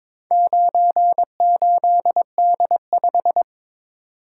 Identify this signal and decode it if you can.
Morse code, heard as a keyed tone, about 22 wpm, 700 Hz. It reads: 98D5
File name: practice / 22wpm / Four Letter-Number Combinations - 22wpm